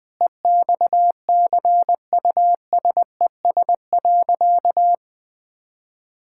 Morse code, heard as a keyed tone, about 21 wpm, 700 Hz. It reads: EXCUSES.
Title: Sentences with 5-Word Phrases - 20wpm